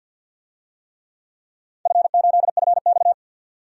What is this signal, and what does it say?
Morse code, heard as a keyed tone, about 51 wpm, 700 Hz. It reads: F8FX
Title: DXCC-100 Call Signs - 4 Characters - 50wpm